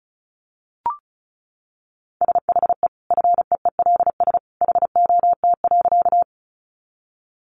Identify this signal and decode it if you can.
Morse code, heard as a keyed tone, about 34 wpm, 700 Hz. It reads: SHE FEELS HOT.
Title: Additional Easy Phrases 35wpm